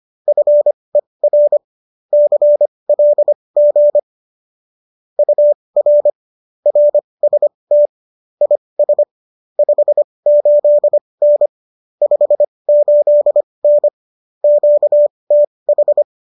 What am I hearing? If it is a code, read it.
Morse code, heard as a keyed tone, about 25 wpm, spaced as for 20 wpm, 590 Hz. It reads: FER CLG UR RST IS 58N 58N QTH